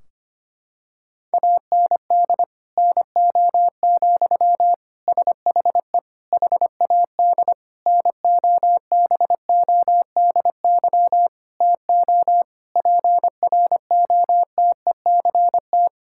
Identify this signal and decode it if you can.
Morse code, heard as a keyed tone, about 25 wpm, 700 Hz. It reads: AND NO, SHE HAD NOBODY TO PROTECT
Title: Sentences from Top 1600 Words 25wpm